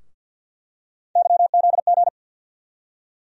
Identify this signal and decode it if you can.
Morse code, heard as a keyed tone, about 47 wpm, 700 Hz. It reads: YZG
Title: USA Call Sign Postfixes - 3 Letters 50wpm